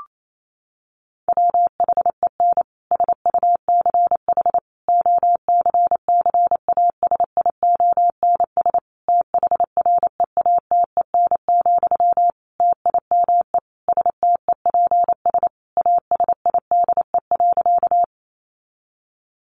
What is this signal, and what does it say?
Morse code, heard as a keyed tone, about 28 wpm, 700 Hz. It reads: WHEN SUCH OCCASIONS THREATEN, TIME STEPS ASIDE.
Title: A Sound Of Thunder 28wpm